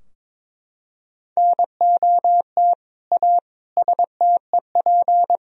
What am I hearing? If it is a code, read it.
Morse code, heard as a keyed tone, about 22 wpm, 700 Hz. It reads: NOT A STEP
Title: Sentences from Top 1000 Words - Review Part 2 - 22wpm